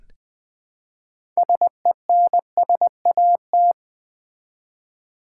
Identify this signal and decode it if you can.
Morse code, heard as a keyed tone, about 20 wpm, 700 Hz. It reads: SENSAT
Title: Build A Word - 3 Letters At A Time - 9-Letter Words - 20wpm